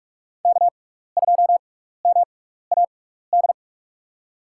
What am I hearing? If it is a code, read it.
Morse code, heard as a keyed tone, about 44 wpm, spaced as for 15 wpm, 700 Hz. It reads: K2MAD
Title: USA Call Signs - 5 Characters - 6x Character Spacing - 45wpm